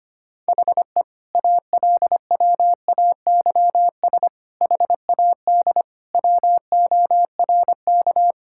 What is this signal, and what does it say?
Morse code, heard as a keyed tone, about 25 wpm, 700 Hz. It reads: HE ALWAYS HAD WORK
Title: Sentences from Top 1600 Words 25wpm